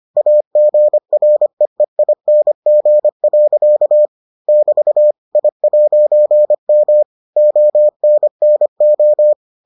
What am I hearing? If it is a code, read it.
Morse code, heard as a keyed tone, about 25 wpm, 600 Hz. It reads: AGREEING. <BT> I'M ONNO